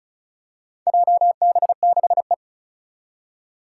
Morse code, heard as a keyed tone, about 36 wpm, 690 Hz. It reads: JB6E
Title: Four Challenging Letter-Number Combinations 35wpm